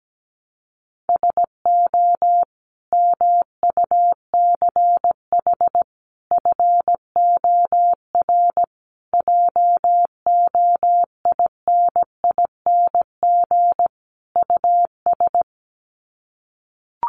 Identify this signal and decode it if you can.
Morse code, heard as a keyed tone, about 17 wpm, 700 Hz. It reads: SO MUCH FOR JOINING US
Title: Sets of 5 Words 17wpm